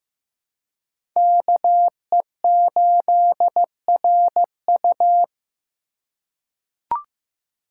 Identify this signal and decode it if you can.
Morse code, heard as a keyed tone, about 15 wpm, 700 Hz. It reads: KE8RU
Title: K1USN SST QSO Practice - 15wpm